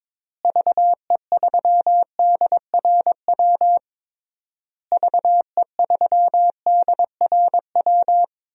Morse code, heard as a keyed tone, about 22 wpm, 700 Hz. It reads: VE3DRW VE3DRW
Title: DXCC-100 Call Signs - 6 Characters - Repeated 3x - 22wpm